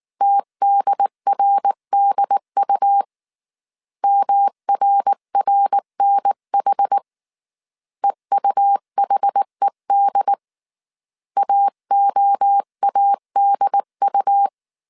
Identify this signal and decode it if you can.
Morse code, heard as a keyed tone, about 19 wpm, 800 Hz. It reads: TDRDU MRRNH EUHED AOADU